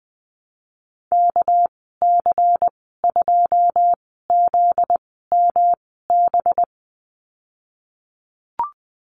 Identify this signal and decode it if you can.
Morse code, heard as a keyed tone, about 20 wpm, spaced as for 15 wpm, 700 Hz. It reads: KC2ZMB
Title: 245 Lesson 39g - Letter z - Call Signs 20wpm